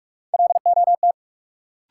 Morse code, heard as a keyed tone, about 43 wpm, 700 Hz. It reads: LOT
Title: Sets of 2 Words - New Word Drill - 45wpm